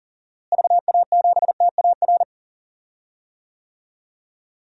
Morse code, heard as a keyed tone, about 40 wpm, 700 Hz. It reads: VA7TAR
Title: DXCC-100 Call Signs - 6 Characters - Repeated 3x - 40wpm